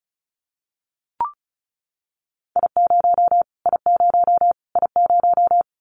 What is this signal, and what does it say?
Morse code, heard as a keyed tone, about 34 wpm, 700 Hz. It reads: I0 I0 I0
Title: Callsign Prefixes for Top 25 DXCC - Repeated 3x - 35wpm